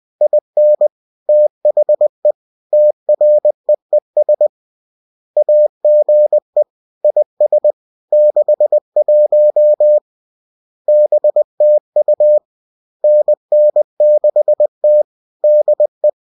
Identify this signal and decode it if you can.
Morse code, heard as a keyed tone, about 20 wpm, 600 Hz. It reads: IN THE TREES AGE IS 61 BTU NN6T DE